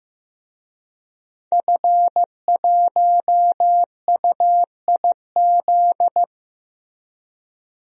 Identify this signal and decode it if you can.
Morse code, heard as a keyed tone, about 15 wpm, 700 Hz. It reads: F1UIZ
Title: DXCC-100 Call Signs - 5 Characters - Repeated 2x - 15wpm